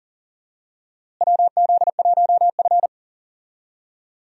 Morse code, heard as a keyed tone, about 37 wpm, 700 Hz. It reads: WZ1F
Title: USA Call Signs - 4 Characters - Repeated 3x - 40wpm